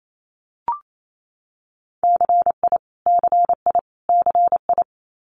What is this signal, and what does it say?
Morse code, heard as a keyed tone, about 28 wpm, 700 Hz. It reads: CI CI CI
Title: Multi-letter Phonemes Repeated 3x 28wpm